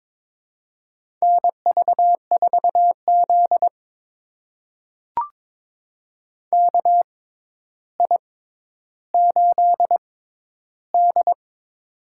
Morse code, heard as a keyed tone, about 22 wpm, spaced as for 11 wpm, 700 Hz. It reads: NV4Z KI8D
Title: USA Call Signs - 6x Character Spacing - 22wpm